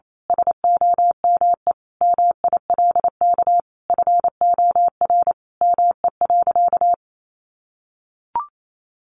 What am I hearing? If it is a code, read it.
Morse code, heard as a keyed tone, about 28 wpm, 700 Hz. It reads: SOME MILK FOR ME.
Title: Sentences from Top 1500 Words 28wpm